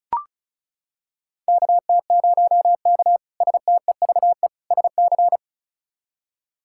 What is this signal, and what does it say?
Morse code, heard as a keyed tone, about 36 wpm, 700 Hz. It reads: KT0K STEVE SC
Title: K1USN SST QSO Practice - 35wpm